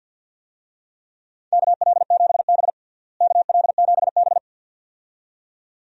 Morse code, heard as a keyed tone, about 49 wpm, 700 Hz. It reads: KL7B KL7B
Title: DXCC-100 Call Signs - 4 Characters - Repeated 2x - 50wpm